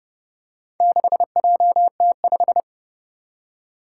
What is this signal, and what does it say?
Morse code, heard as a keyed tone, about 30 wpm, 690 Hz. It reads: 6JT5
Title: Four Challenging Letter-Number Combinations 30wpm